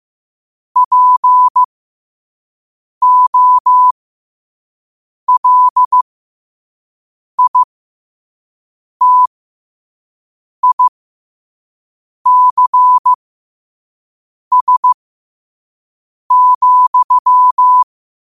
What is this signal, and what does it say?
Morse code, heard as a keyed tone, about 15 wpm, spaced as for 5 wpm, 1000 Hz. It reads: POLITICS,
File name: qotdfiles / 5wpm